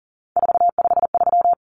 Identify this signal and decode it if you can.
Morse code, heard as a keyed tone, about 40 wpm, 700 Hz. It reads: 453
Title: Common Cut Numbers - 3-Digit 40wpm